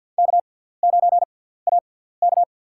Morse code, heard as a keyed tone, about 52 wpm, spaced as for 17 wpm, 700 Hz. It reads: K9AK
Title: USA Call Signs - 4 Characters - 6x Character Spacing - 50wpm